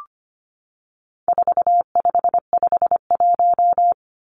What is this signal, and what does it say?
Morse code, heard as a keyed tone, about 25 wpm, 700 Hz. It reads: 4551
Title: Numbers - 4-Digit 25wpm